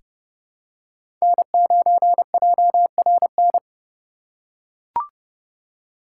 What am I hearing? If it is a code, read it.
Morse code, heard as a keyed tone, about 32 wpm, 700 Hz. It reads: N9JRN
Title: USA Call Signs - 5 Characters - 2x Character Spacing - 30wpm